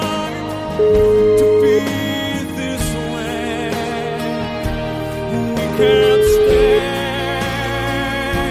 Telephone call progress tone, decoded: ringback tone

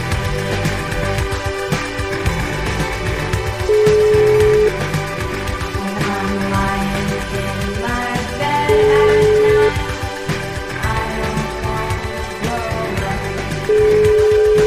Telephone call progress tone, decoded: ringback tone

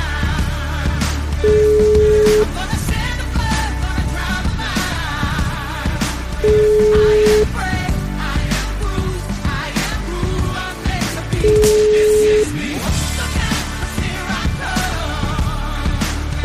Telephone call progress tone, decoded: ringback tone

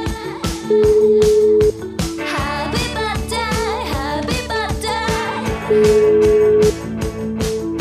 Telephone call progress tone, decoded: ringback tone

